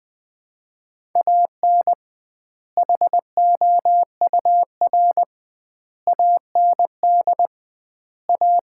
Morse code, heard as a keyed tone, about 20 wpm, 700 Hz. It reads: AN HOUR AND A
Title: 2x Word-Spacing - Sets of 5 Words - 20wpm